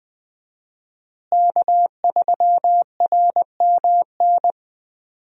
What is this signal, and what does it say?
Morse code, heard as a keyed tone, about 21 wpm, 700 Hz. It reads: K3RMN
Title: USA Call Signs - 5 Characters - Repeated 3x - 20wpm